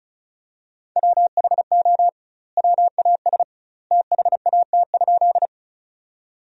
Morse code, heard as a keyed tone, about 34 wpm, 700 Hz. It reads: WHO WAS THAT?
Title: 2x Word-Spacing - Sentences from Top 100 Words - 35wpm